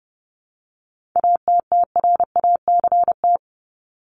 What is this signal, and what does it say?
Morse code, heard as a keyed tone, about 30 wpm, 700 Hz. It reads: ATTRACT
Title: Top 1501-1750 Words 30wpm